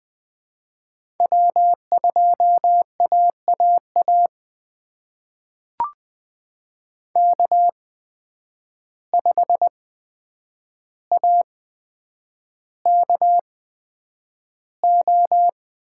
Morse code, heard as a keyed tone, about 21 wpm, spaced as for 9 wpm, 700 Hz. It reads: W2AAA K5AKO
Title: USA Call Signs - 5 Characters - 8x Character Spacing - 20wpm